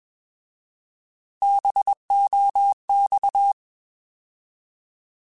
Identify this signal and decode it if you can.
Morse code, heard as a keyed tone, about 21 wpm, 800 Hz. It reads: BOX
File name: Alphabetical / Alphabetical mp3s / Bs @20